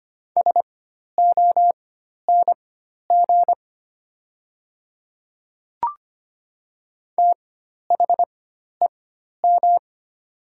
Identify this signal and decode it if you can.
Morse code, heard as a keyed tone, about 25 wpm, spaced as for 12 wpm, 700 Hz. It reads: SONG THEM
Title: 4x Character Spacing - Top 500 Words - 25wpm